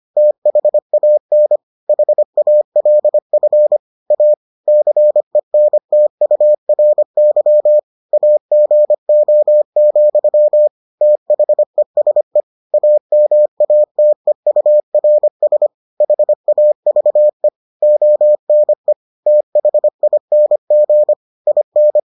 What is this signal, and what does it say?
Morse code, heard as a keyed tone, about 25 wpm, 600 Hz. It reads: THAN HALF A CENTURY AGO, THESE AMATEURS HAVE ONE THING IN